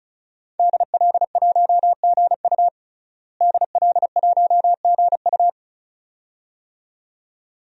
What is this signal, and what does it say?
Morse code, heard as a keyed tone, about 35 wpm, 700 Hz. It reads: DL1GU DL1GU
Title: DXCC-100 Call Signs - 5 Characters - Repeated 2x - 35wpm